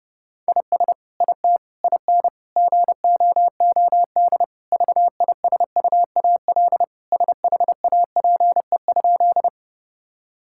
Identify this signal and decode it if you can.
Morse code, heard as a keyed tone, about 30 wpm, 700 Hz. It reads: IS IT IN GOOD VISUAL SHAPE?
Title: Sentences from Top 1000 Words - Review Part 6 - 30wpm